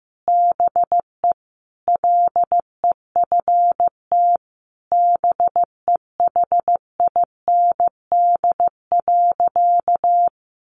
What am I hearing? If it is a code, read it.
Morse code, heard as a keyed tone, about 15 wpm, 700 Hz. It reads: BE LEFT BEHIND.
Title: A Sound Of Thunder 15wpm